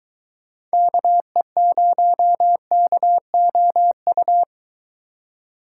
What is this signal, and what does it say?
Morse code, heard as a keyed tone, about 22 wpm, 700 Hz. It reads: KE0KOU